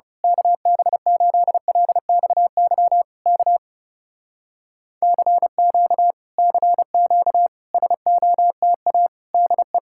Morse code, heard as a keyed tone, about 29 wpm, 700 Hz. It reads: KB8LXY K CQ CQ SOTA DE
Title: Speed-Racing Common QSO Phrases Encore - 30wpm